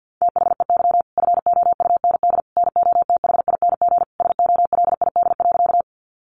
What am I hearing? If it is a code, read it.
Morse code, heard as a keyed tone, about 50 wpm, 700 Hz. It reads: THEY FOUND NOTHING SOLID.
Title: Sentences from Top 1500 Words 50wpm